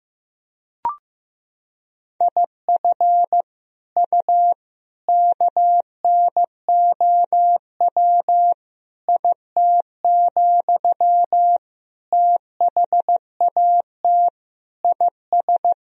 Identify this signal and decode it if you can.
Morse code, heard as a keyed tone, about 15 wpm, 700 Hz. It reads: IF U KNOW IT, THAT IS
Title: Sentences from Top 400 Words and Abbreviations 15wpm